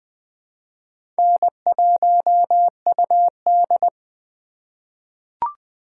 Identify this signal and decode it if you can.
Morse code, heard as a keyed tone, about 20 wpm, 700 Hz. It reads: N1UD
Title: USA Call Signs - 4 Characters - 3x Character Spacing - 20wpm